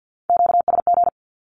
Morse code, heard as a keyed tone, about 50 wpm, 700 Hz. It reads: QSG
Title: Q-Codes with Spoken Meaning - 50wpm